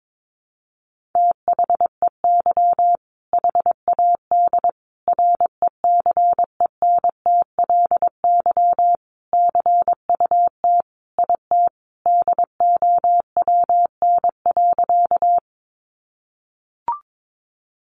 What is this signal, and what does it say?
Morse code, heard as a keyed tone, about 22 wpm, 700 Hz. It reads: THEY HAD RECENTLY CUT IT DOWN.
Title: Sentences from Top 1200 Words 22wpm